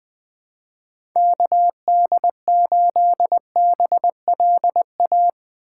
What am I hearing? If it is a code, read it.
Morse code, heard as a keyed tone, about 21 wpm, 700 Hz. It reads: KD8BLA
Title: USA Call Signs - 2x Character Spacing - 20wpm